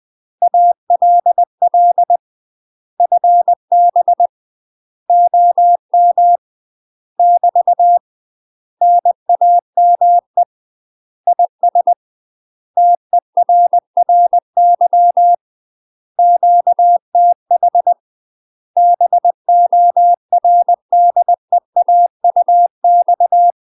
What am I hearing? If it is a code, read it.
Morse code, heard as a keyed tone, about 20 wpm, 700 Hz. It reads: ALL FB OM <BT> NAME IS TERRY QTH BORDEAUX
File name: UK MASTER QSOs TRAINING / 20WPM / UK MASTER QSOs 0015